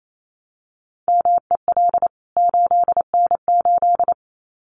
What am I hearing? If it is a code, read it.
Morse code, heard as a keyed tone, about 28 wpm, 700 Hz. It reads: MEL 8N8
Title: CWT Exchange 28wpm